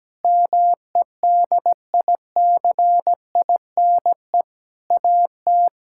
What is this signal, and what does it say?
Morse code, heard as a keyed tone, about 17 wpm, 700 Hz. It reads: MEDICINE AT
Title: Sentences from Top 1000 Words - Review Part 6 - 17wpm